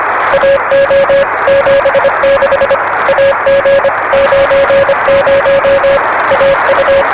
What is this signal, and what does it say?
Morse code, heard as a keyed tone, about 25 wpm, 560 Hz, with noise. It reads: AO 76 AG 90 AU
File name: RS-7 Beacon 16 Feb 1985 oz6bl